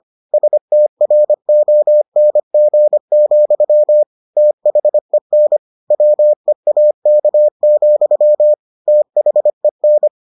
Morse code, heard as a keyed tone, about 25 wpm, 600 Hz. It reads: STRONG, THEN WEAK, THEN